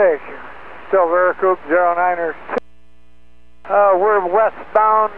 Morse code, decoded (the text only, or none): none